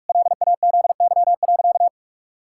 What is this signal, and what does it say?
Morse code, heard as a keyed tone, about 41 wpm, 700 Hz. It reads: LAZY.
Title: Pangram Challenge v2 - 45wpm